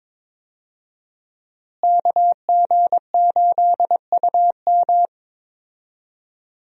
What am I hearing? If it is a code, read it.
Morse code, heard as a keyed tone, about 22 wpm, 700 Hz. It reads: KG8UM
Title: USA Call Signs - 5 Characters - Repeated 2x - 22wpm